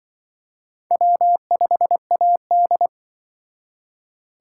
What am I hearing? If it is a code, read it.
Morse code, heard as a keyed tone, about 25 wpm, 700 Hz. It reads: W5AD
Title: Speed Racing - USA Call Signs 20wpm